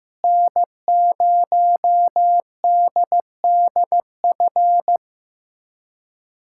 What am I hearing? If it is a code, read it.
Morse code, heard as a keyed tone, about 15 wpm, 700 Hz. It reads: N0DDF
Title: USA Callsign DE Callsign 15wpm